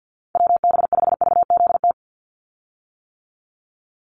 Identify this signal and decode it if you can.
Morse code, heard as a keyed tone, about 50 wpm, 700 Hz. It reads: WB5VZT
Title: USA Call Signs 50wpm